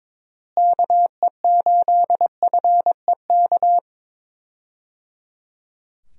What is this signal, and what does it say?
Morse code, heard as a keyed tone, about 22 wpm, 700 Hz. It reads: KE8FEK
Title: USA Call Signs - 6 Characters - Repeated 3x - 22wpm